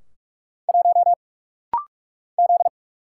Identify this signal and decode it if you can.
Morse code, heard as a keyed tone, about 46 wpm, 700 Hz. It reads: 1 Z
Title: Single Letter-Number Alt - Rapid-Fire 45wpm